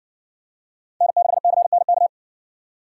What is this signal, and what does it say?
Morse code, heard as a keyed tone, about 60 wpm, 700 Hz. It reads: N6CNK